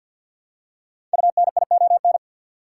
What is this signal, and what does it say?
Morse code, heard as a keyed tone, about 51 wpm, 700 Hz. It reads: UNION
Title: Five Letter Words - 50wpm